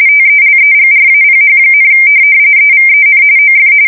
RTTY, about 45.45 baud. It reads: TELEGRAPH, EVEN OF
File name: fsk170 / morkrum